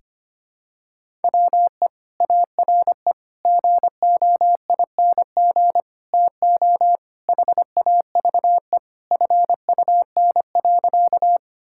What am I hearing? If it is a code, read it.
Morse code, heard as a keyed tone, about 25 wpm, 700 Hz. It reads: WE ARE GOING TO HAVE FUN.
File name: practice / 25wpm / Sentences with 4-Word Phrases - 25wpm